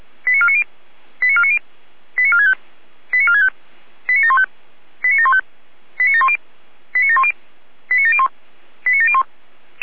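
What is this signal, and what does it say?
Five-tone selective call (ZVEI), heard as quick runs of five tones, tones of about 70 ms. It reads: 89490 89490 89465 89465 89715 89715 89710 89710 89701 89701
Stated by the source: ZVEI five-tone call received by SDR.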